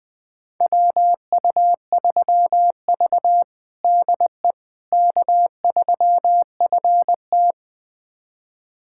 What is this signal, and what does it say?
Morse code, heard as a keyed tone, about 21 wpm, 700 Hz. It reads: WU3V DE K3FT
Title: USA Callsign DE Callsign - 4 Characters - 20wpm